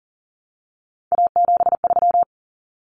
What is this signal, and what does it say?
Morse code, heard as a keyed tone, about 40 wpm, 700 Hz. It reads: A73
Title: Common Cut Numbers - 3-Digit 40wpm